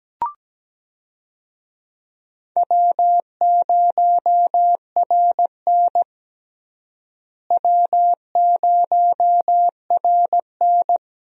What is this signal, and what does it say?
Morse code, heard as a keyed tone, about 17 wpm, 700 Hz. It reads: W0RN W0RN
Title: USA Call Signs - 4 Characters - Repeated 3x - 17wpm